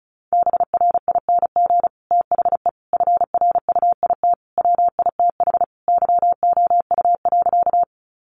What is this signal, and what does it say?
Morse code, heard as a keyed tone, about 34 wpm, 700 Hz. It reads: BRING THE FRUIT WITH YOU.
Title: Sentences from Top 1300 Words 35wpm